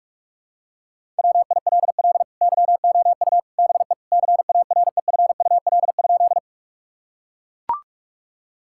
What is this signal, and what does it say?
Morse code, heard as a keyed tone, about 41 wpm, 700 Hz. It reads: WILL YOU BE CAREFUL?
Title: Sentences from Top 1600 Words 45wpm